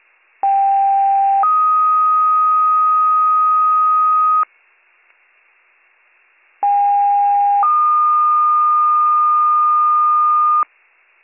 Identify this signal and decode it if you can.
Two-tone page, heard as a one-second tone, then a three-second tone, about 790 then 1250 Hz, about 810 then 1220 Hz. It is heard from 0.4 to 10.6 s.